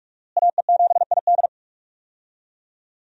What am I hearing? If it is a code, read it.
Morse code, heard as a keyed tone, about 46 wpm, 700 Hz. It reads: AE7ID
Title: USA Call Signs - 5 Characters - Repeated 3x - 45wpm